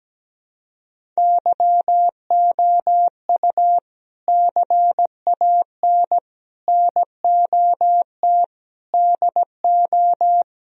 Morse code, heard as a keyed tone, about 17 wpm, 700 Hz. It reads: YOU CAN NOT DO
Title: Speed-Racing - Sets of 5 Words 17wpm